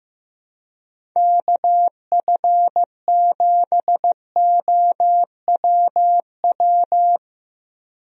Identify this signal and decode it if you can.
Morse code, heard as a keyed tone, about 15 wpm, 700 Hz. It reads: KF7OWW
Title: USA Call Signs - 6 Characters - Repeated 3x - 15wpm